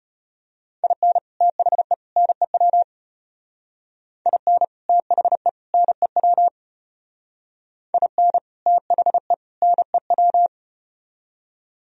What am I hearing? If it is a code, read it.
Morse code, heard as a keyed tone, about 32 wpm, 700 Hz. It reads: IN THE NEW IN THE NEW IN THE NEW